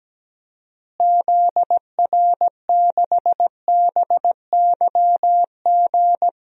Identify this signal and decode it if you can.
Morse code, heard as a keyed tone, about 17 wpm, 700 Hz. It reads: ZR6BYG
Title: DXCC-100 Call Signs - 6 Characters - Repeated 2x - 17wpm